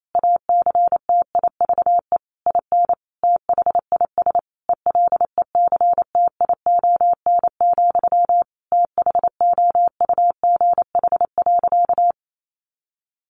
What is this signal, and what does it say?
Morse code, heard as a keyed tone, about 28 wpm, 700 Hz. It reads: ACTIVE IN THIS ELECTION, THOUGH.
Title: Sentences from Top 1500 Words 28wpm